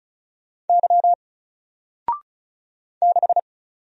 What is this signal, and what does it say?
Morse code, heard as a keyed tone, about 35 wpm, 700 Hz. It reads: Y 6
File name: Single-Letter-Number-35wpm